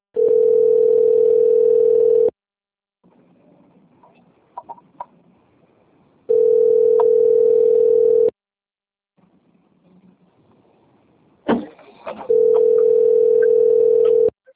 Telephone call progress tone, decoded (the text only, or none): ringback tone